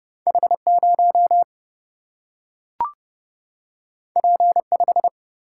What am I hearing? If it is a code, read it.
Morse code, heard as a keyed tone, about 30 wpm, 700 Hz. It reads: H 0 P 5
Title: Single-Letter Single-Number 30wpm